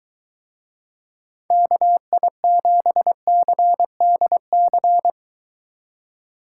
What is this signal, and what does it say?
Morse code, heard as a keyed tone, about 23 wpm, 700 Hz. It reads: KI7CDC